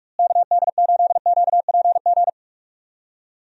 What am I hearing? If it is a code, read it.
Morse code, heard as a keyed tone, about 46 wpm, 700 Hz. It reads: KD8QPG
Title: USA Call Signs - 6 Characters - Repeated 3x - 45wpm